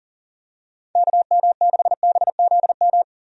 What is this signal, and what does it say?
Morse code, heard as a keyed tone, about 40 wpm, 700 Hz. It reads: KM6BZM
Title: USA Call Signs - 6 Characters - 40wpm